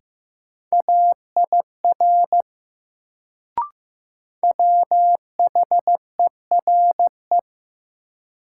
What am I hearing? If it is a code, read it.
Morse code, heard as a keyed tone, about 15 wpm, 700 Hz. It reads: AIR WHERE
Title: Sentences from Top 200 Words - New Word Drill - 15wpm